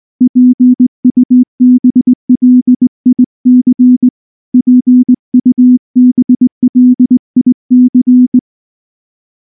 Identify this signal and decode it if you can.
Morse code, heard as a keyed tone, about 19 wpm, 250 Hz. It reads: PUBLIC PUBLIC
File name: morse / morse-training-20WPM-10